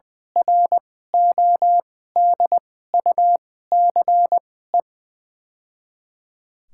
Morse code, heard as a keyed tone, about 20 wpm, spaced as for 14 wpm, 700 Hz. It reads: RODUCE
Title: 2x Character Spacing - Top 500 Words - 20wpm